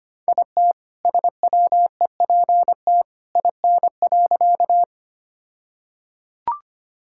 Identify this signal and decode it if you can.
Morse code, heard as a keyed tone, about 25 wpm, 700 Hz. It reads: IT SWEPT IN.
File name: practice / 25wpm / All-Age Appropriate Jokes - 25wpm